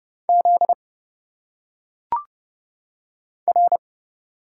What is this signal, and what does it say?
Morse code, heard as a keyed tone, about 30 wpm, 700 Hz. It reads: Z R
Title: Single Letters - Phonetic Pronunciation 30wpm